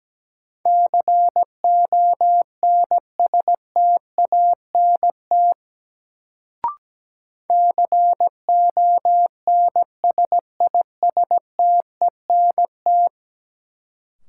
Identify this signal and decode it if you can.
Morse code, heard as a keyed tone, about 17 wpm, 700 Hz. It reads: CONSTANT CONSISTENT
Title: Sentences from Top 1500 Words - New Word Drill Alt - 17wpm